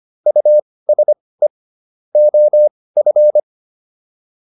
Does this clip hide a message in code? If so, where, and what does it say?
Yes: Morse code from 0.3 to 3.4 s, heard as a keyed tone, about 25 wpm, spaced as for 18 wpm, 600 Hz. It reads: USE OF